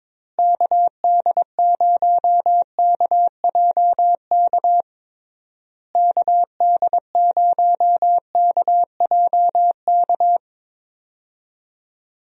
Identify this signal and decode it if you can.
Morse code, heard as a keyed tone, about 22 wpm, 700 Hz. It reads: KD0KJK KD0KJK
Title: USA Call Signs - 6 Characters - Repeated 3x - 22wpm